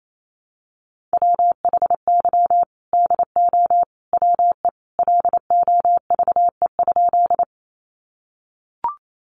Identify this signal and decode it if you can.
Morse code, heard as a keyed tone, about 28 wpm, 700 Hz. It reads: WHY DO WE LOVE?
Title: Sentences from Top 400 Words 28wpm